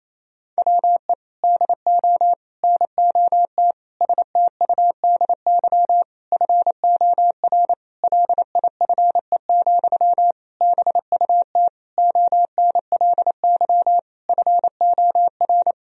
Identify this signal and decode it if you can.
Morse code, heard as a keyed tone, about 28 wpm, 700 Hz. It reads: WE DO NOT STUDY FOR LIFE, BUT ONLY FOR
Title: Sentences from Top 1000 Words - Review Part 1 - 28wpm